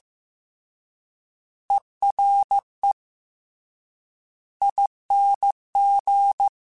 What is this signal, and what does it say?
Morse code, heard as a keyed tone, about 15 wpm, 800 Hz. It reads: ERE ING
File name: Special INGs / Special INGs-mp3s / Ere ing @13